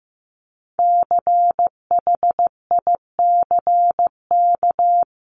Morse code, heard as a keyed tone, about 15 wpm, 700 Hz. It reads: CHICK
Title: CWT Exchange 15wpm